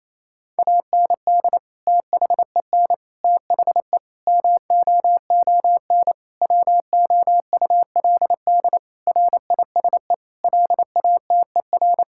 Morse code, heard as a keyed tone, about 28 wpm, 700 Hz. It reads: AND THEN THE MOON WOULD RISE LATER
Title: Sentences from Top 1000 Words - Review Part 6 - 28wpm